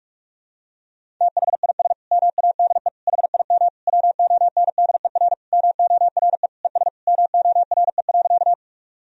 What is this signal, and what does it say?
Morse code, heard as a keyed tone, about 46 wpm, 700 Hz. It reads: THIS MADE HIM WONDER MORE ES MORE.
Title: Sentences from Top 500 Words and Abbreviations 45wpm